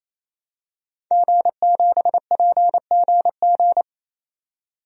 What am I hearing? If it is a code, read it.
Morse code, heard as a keyed tone, about 28 wpm, 700 Hz. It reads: G7PGG
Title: DXCC-100 Call Signs - 5 Characters - 28wpm